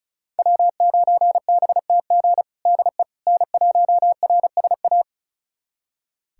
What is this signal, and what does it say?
Morse code, heard as a keyed tone, about 36 wpm, 700 Hz. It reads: W9BTG DE N1RSA
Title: USA Callsign DE Callsign - 5 Characters - 35wpm